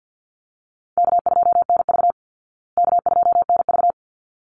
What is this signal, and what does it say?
Morse code, heard as a keyed tone, about 50 wpm, 700 Hz. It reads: K2NV K2NV
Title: Easy Call Signs Repeated 2x - 50wpm